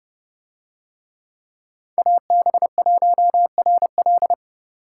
Morse code, heard as a keyed tone, about 28 wpm, 700 Hz. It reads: AB1RL